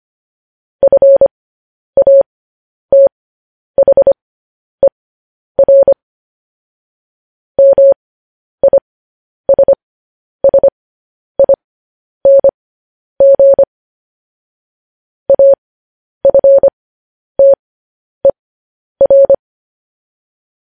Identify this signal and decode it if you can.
Morse code, heard as a keyed tone, about 25 wpm, spaced as for 10 wpm, 550 Hz. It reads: FATHER MISSING AFTER